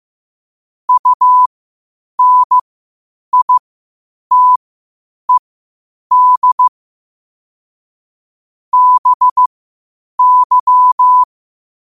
Morse code, heard as a keyed tone, about 15 wpm, spaced as for 8 wpm, 1000 Hz. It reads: UNITED BY
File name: qotdfiles / 8wpm